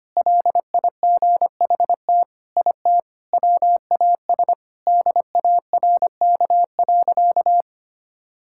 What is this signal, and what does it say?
Morse code, heard as a keyed tone, about 25 wpm, 700 Hz. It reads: LIGHT IT WAS DARK.
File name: practice / 25wpm / Sentences with 4-Word Phrases - 25wpm